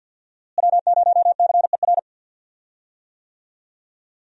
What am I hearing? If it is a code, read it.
Morse code, heard as a keyed tone, about 50 wpm, 700 Hz. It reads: W0CER
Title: USA Call Signs - 5 Characters - Repeated 3x - 50wpm